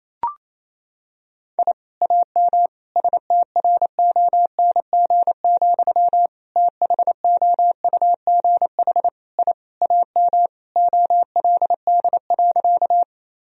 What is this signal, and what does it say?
Morse code, heard as a keyed tone, about 28 wpm, 700 Hz. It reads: I AM STRONG, THOUGH I AM OLD.
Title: Sentences from Top 1000 Words - Review Part 3 - 28wpm